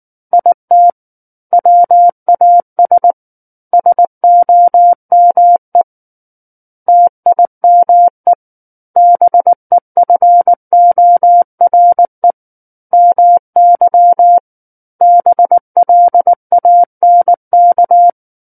Morse code, heard as a keyed tone, about 19 wpm, 700 Hz. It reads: IT WAS SOME TIME BEFORE MY BLANK
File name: Book One / 10-Chapter 10 - 19 WPM